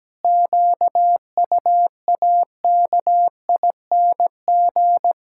Morse code, heard as a keyed tone, about 17 wpm, 700 Hz. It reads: QUAKING
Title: Pangram Challenge v2 - 17wpm